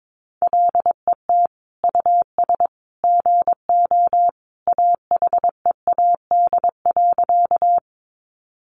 Morse code, heard as a keyed tone, about 22 wpm, 700 Hz. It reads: LET US GO AHEAD.
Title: Sentences from Top 1100 Words 22wpm